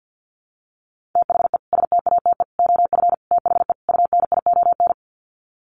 Morse code, heard as a keyed tone, about 48 wpm, 700 Hz. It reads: THE STATE OF THE UNION
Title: Sets of 5 Words Encore 50wpm